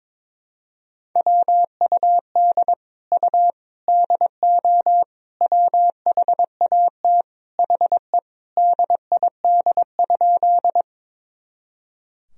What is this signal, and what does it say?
Morse code, heard as a keyed tone, about 22 wpm, 700 Hz. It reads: WUD U DO WHAT HE DID?
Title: Sentences from Top 100 Words and Abbreviations 22wpm